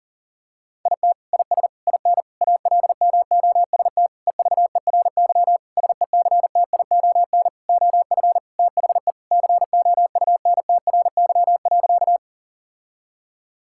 Morse code, heard as a keyed tone, about 40 wpm, 700 Hz. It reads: IT IS IN ALMOST EVERY SECTION OF THE COUNTRY.
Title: Sentences from Top 1000 Words - Review Part 5 - 40wpm